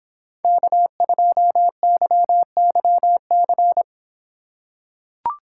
Morse code, heard as a keyed tone, about 26 wpm, 700 Hz. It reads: K2YYC